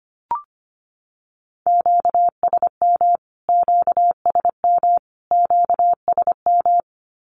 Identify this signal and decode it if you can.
Morse code, heard as a keyed tone, about 25 wpm, 700 Hz. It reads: QSM QSM QSM
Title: Q-Codes Repeated 3x 25wpm